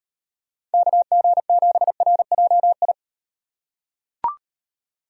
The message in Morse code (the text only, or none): KG7RJI